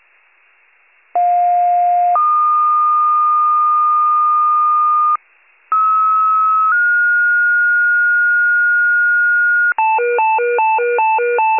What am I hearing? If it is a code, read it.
Two-tone page, heard as a one-second tone, then a three-second tone, about 710 then 1220 Hz, about 1290 then 1430 Hz, with noise. It is heard from 1.2 to 9.7 s.